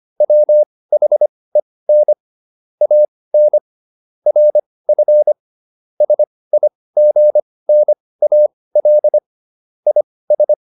Morse code, heard as a keyed tone, about 25 wpm, spaced as for 18 wpm, 600 Hz. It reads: WHEN AN RF SIGNAL IS